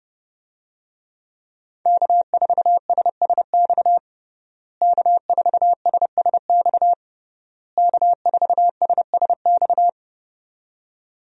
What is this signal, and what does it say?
Morse code, heard as a keyed tone, about 32 wpm, 700 Hz. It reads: K4SSX K4SSX K4SSX
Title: USA Call Signs - 5 Characters - Repeated 3x - 30wpm